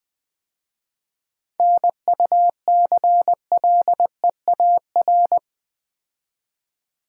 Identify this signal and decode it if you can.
Morse code, heard as a keyed tone, about 21 wpm, 700 Hz. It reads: NUCLEAR
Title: 7 Letter Words - First 3 Letters Given - 20wpm